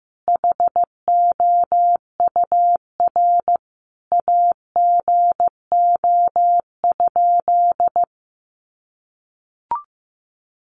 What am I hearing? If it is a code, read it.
Morse code, heard as a keyed tone, about 15 wpm, 700 Hz. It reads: HOUR AGO?
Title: Sentences from Top 1300 Words 15wpm